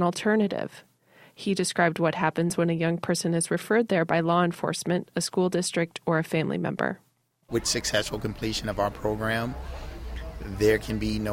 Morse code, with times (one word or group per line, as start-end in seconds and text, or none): none